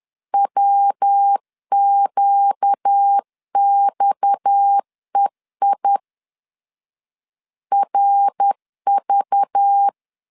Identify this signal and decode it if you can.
Morse code, heard as a keyed tone, about 10 wpm, 800 Hz. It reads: WQXEI RV
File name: CW / DET / 1 6 1500C S B 10wpm 4137